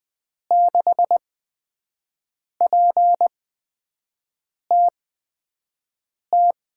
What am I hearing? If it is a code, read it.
Morse code, heard as a keyed tone, about 20 wpm, spaced as for 5 wpm, 700 Hz. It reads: 6PTT
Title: USA Call Signs - 8x Character Spacing - 20wpm